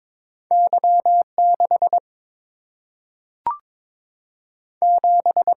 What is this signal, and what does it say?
Morse code, heard as a keyed tone, about 22 wpm, 690 Hz. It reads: Y6 7
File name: practice / 22wpm / Two Letter-Number Combinations - 22wpm